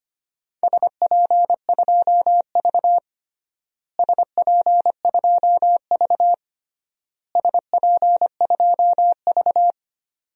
Morse code, heard as a keyed tone, about 25 wpm, 700 Hz. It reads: SP2V SP2V SP2V
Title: DXCC-100 Call Signs - 4 Characters - Repeated 3x - 25wpm